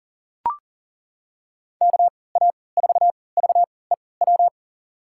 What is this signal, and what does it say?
Morse code, heard as a keyed tone, about 40 wpm, spaced as for 23 wpm, 700 Hz. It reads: KA4VEW
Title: USA Call Signs - 6 Characters - 3x Character Spacing - 40wpm